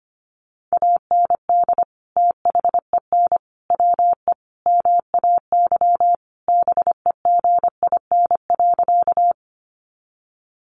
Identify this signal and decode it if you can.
Morse code, heard as a keyed tone, about 25 wpm, 700 Hz. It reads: AND THEN WE MAY BEGIN.
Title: Sentences from Top 300 Words 25wpm